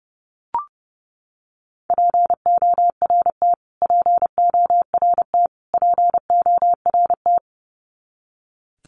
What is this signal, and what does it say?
Morse code, heard as a keyed tone, about 30 wpm, 700 Hz. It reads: PORT PORT PORT
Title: Top 200 Words Repeated 30wpm